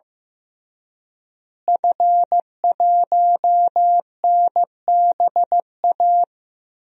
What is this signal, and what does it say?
Morse code, heard as a keyed tone, about 15 wpm, 700 Hz. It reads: F1NBA
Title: DXCC-100 Call Signs - 5 Characters - Repeated 2x - 15wpm